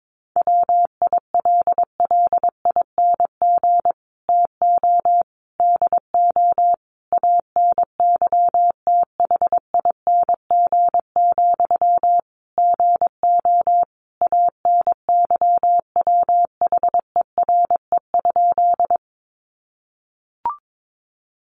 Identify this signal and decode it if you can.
Morse code, heard as a keyed tone, about 22 wpm, 700 Hz. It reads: WILLING TO DO ANYTHING, GO ANYWHERE?
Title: Sentences from Top 1400 Words 22wpm